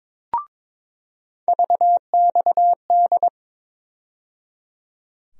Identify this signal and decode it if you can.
Morse code, heard as a keyed tone, about 22 wpm, 700 Hz. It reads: VXD
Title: USA Call Sign Postfixes - 3 Letters 22wpm